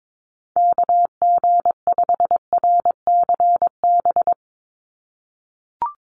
Morse code, heard as a keyed tone, about 22 wpm, 700 Hz. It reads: KG5RCB
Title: USA Call Signs 22wpm